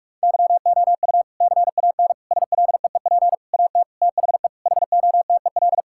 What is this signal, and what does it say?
Morse code, heard as a keyed tone, about 46 wpm, 700 Hz. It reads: YOU CAN SLEEP AT THE HOTEL
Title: Sentences from Top 1600 Words 45wpm